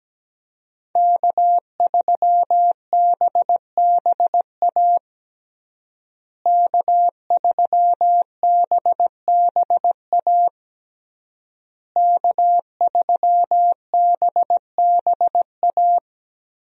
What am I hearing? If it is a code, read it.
Morse code, heard as a keyed tone, about 17 wpm, 700 Hz. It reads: K3BBA K3BBA K3BBA
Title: USA Call Signs - 5 Characters - Repeated 3x - 17wpm